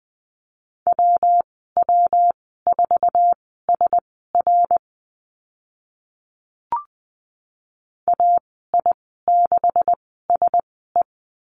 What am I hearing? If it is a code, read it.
Morse code, heard as a keyed tone, about 20 wpm, spaced as for 14 wpm, 700 Hz. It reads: WW4SR AI6SE
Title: 154 Lesson 25g - Letter p - Call Signs 20wpm